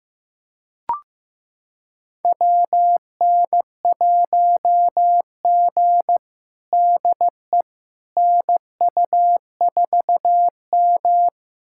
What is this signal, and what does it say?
Morse code, heard as a keyed tone, about 15 wpm, 700 Hz. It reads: WN1G DE NU4M
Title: USA Callsign DE Callsign - 4 Characters - 15wpm